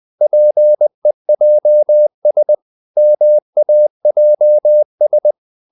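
Morse code, heard as a keyed tone, about 20 wpm, 600 Hz. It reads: PEJS MAJS
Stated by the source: Morse code practice of short Danish words at 20 wpm, from CW med Gnister.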